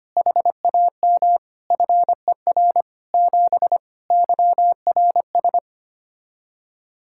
Morse code, heard as a keyed tone, about 25 wpm, 700 Hz. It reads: HAM FER 7 YRS
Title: K1USN SST QSO Practice 25wpm